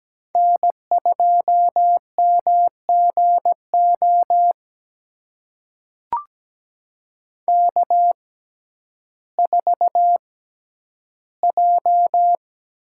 Morse code, heard as a keyed tone, about 17 wpm, spaced as for 10 wpm, 700 Hz. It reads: N2MGO K4J
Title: USA Call Signs - 5 Characters - 6x Character Spacing - 17wpm